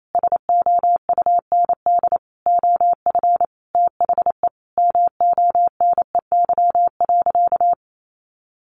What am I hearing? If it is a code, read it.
Morse code, heard as a keyed tone, about 28 wpm, 700 Hz. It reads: SOUND OF THE MONEY.
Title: Sentences from Top 1100 Words 28wpm